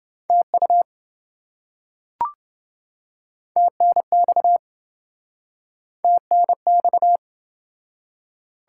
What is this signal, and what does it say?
Morse code, heard as a keyed tone, about 28 wpm, 700 Hz. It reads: TU TNX TNX